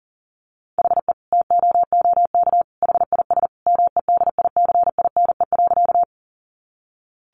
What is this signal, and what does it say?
Morse code, heard as a keyed tone, about 40 wpm, 700 Hz. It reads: HE TOOK HIS MEDICINE.
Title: Sentences from Top 900 Words 40wpm